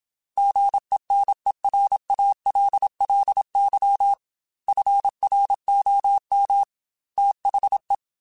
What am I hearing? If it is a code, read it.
Morse code, heard as a keyed tone, about 27 wpm, 800 Hz. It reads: GENERALLY FROM THE
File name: Weather Sets / Weather Sets-mp3s / Weather 5 5 @25